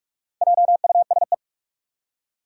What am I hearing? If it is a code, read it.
Morse code, heard as a keyed tone, about 44 wpm, 700 Hz. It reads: JUSE
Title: Four Challenging Letter-Number Combinations 45wpm